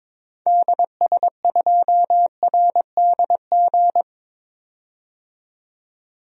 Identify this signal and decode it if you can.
Morse code, heard as a keyed tone, about 22 wpm, 700 Hz. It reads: DS2RDG
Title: DXCC-100 Call Signs - 6 Characters - 22wpm